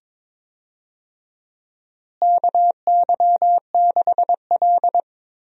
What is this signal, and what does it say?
Morse code, heard as a keyed tone, about 22 wpm, 700 Hz. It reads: KY6L
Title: USA Call Signs - 4 Characters - Repeated 2x - 22wpm